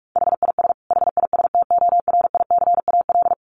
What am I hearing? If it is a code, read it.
Morse code, heard as a keyed tone, about 44 wpm, 700 Hz. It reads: HIS HISTORICAL
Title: Sentences from Top 1300 Words 45wpm